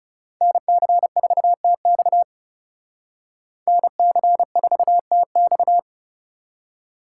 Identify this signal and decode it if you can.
Morse code, heard as a keyed tone, about 38 wpm, 700 Hz. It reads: NC4TX NC4TX